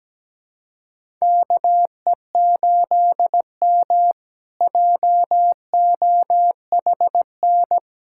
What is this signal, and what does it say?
Morse code, heard as a keyed tone, about 17 wpm, 700 Hz. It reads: KE8M JOHN
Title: K1USN SST QSO Practice - 17wpm